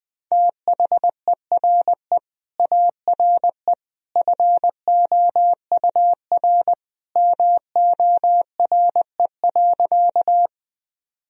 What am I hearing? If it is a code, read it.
Morse code, heard as a keyed tone, about 20 wpm, 700 Hz. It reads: THERE ARE FOUR MORE.
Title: Sentences from Top 1000 Words - Review Part 1 - 20wpm